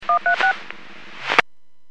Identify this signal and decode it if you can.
DTMF, heard as telephone touch tones, tones of about 110 ms. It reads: 136